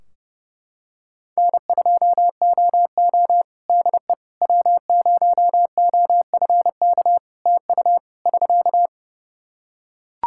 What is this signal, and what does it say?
Morse code, heard as a keyed tone, about 30 wpm, 700 Hz. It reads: N2OO DE W0OFK TU <SK>
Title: K1USN SST QSO Practice 30wpm